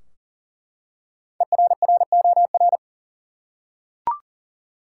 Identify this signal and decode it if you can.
Morse code, heard as a keyed tone, about 37 wpm, 700 Hz. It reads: ERROR
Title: Five Letter Words - 40wpm